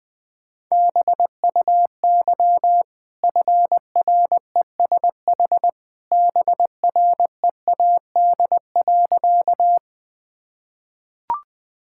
Morse code, heard as a keyed tone, about 21 wpm, 700 Hz. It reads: BUY FRESH BREAD.
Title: Easy Sentences with Hints - 20wpm